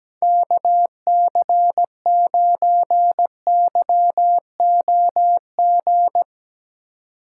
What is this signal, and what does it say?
Morse code, heard as a keyed tone, about 17 wpm, 700 Hz. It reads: KC9YOG
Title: USA Call Signs - 2x Character Spacing - 17wpm